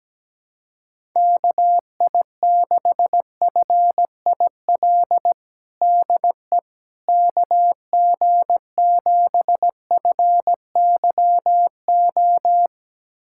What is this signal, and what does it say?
Morse code, heard as a keyed tone, about 17 wpm, 700 Hz. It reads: KI6FIL DE KG7FYO
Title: USA Callsign DE Callsign 17wpm